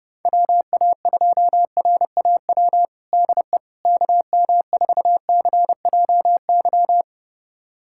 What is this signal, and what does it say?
Morse code, heard as a keyed tone, about 30 wpm, 700 Hz. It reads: WA2RAW DE KM4CJY
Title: USA Callsign DE Callsign - 6 Characters - 30wpm